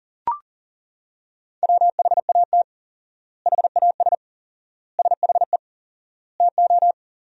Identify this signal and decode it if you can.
Morse code, heard as a keyed tone, about 37 wpm, 700 Hz. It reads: WHAT HAS SHE TO
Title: 4x Word-Spacing - Sentences from Top 100 Words - 40wpm